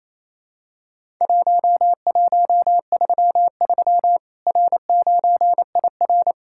Morse code, heard as a keyed tone, about 28 wpm, 700 Hz. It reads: 1133 R9IR
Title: Worked All Europe DX Contest - QTC Exchange - 28wpm